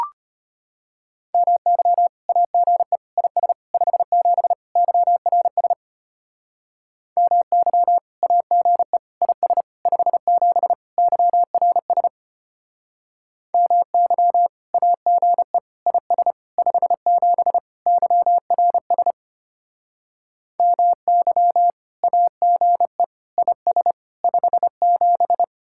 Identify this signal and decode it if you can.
Morse code, heard as a keyed tone, about 32 wpm, 700 Hz. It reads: MY AGE IS 57 YRS MY AGE IS 57 YRS MY AGE IS 57 YRS MY AGE IS 57